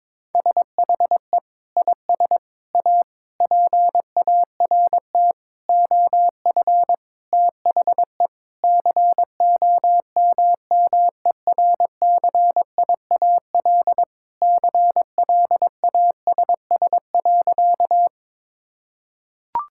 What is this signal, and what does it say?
Morse code, heard as a keyed tone, about 22 wpm, 700 Hz. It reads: SHE IS A PART OF THE COMMERCIAL CLASS.
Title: Sentences from Top 1000 Words - Review Part 5 - 22wpm